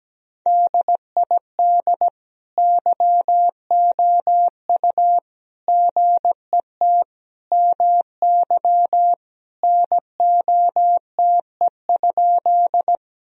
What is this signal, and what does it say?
Morse code, heard as a keyed tone, about 17 wpm, 700 Hz. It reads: DID YOU GET MY NOTE?
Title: Sentences from Top 1000 Words - Review Part 2 - 17wpm